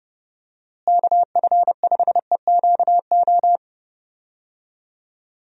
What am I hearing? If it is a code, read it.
Morse code, heard as a keyed tone, about 30 wpm, 700 Hz. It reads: KF5EQO
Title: USA Call Signs - 6 Characters - Repeated 2x - 30wpm